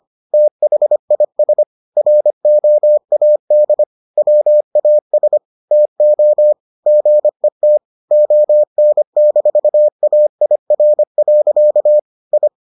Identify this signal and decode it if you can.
Morse code, heard as a keyed tone, about 25 wpm, 600 Hz. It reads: THIS ROAD WAS TO GET ON-AIR. I